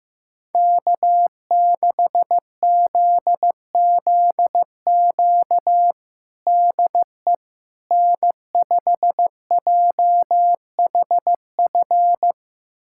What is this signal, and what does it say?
Morse code, heard as a keyed tone, about 15 wpm, 700 Hz. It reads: K6ZZQ DE N5JHF
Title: USA Callsign DE Callsign - 5 Characters - 15wpm